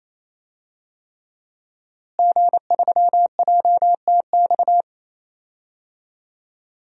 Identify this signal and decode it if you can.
Morse code, heard as a keyed tone, about 27 wpm, 700 Hz. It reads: G3JTX
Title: DXCC-100 Call Signs - 5 Characters - Repeated 3x - 28wpm